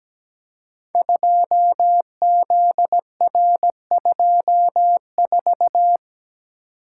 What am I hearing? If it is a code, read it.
Morse code, heard as a keyed tone, about 17 wpm, 700 Hz. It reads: 2ZR24
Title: Code Groups - 5 Letter:Numbers - 17wpm